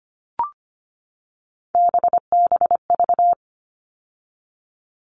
Morse code, heard as a keyed tone, about 25 wpm, 700 Hz. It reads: BBV
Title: All Cut Numbers - 3-Digit 25wpm